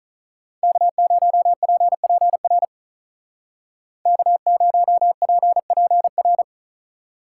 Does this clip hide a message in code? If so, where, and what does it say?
Yes: Morse code from 0.6 to 6.4 s, heard as a keyed tone, about 47 wpm, 700 Hz. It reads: K0PPR K0PPR